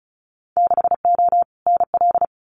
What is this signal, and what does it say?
Morse code, heard as a keyed tone, about 34 wpm, 700 Hz. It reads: 6O NL
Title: Winter Field Day Exchanges Encore 35wpm